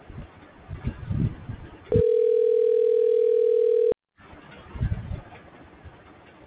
Ringback tone (North America), 440 + 480 Hz.